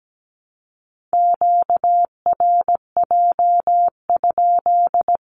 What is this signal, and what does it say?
Morse code, heard as a keyed tone, about 17 wpm, 700 Hz. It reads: QRJ?
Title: Q-Codes 17wpm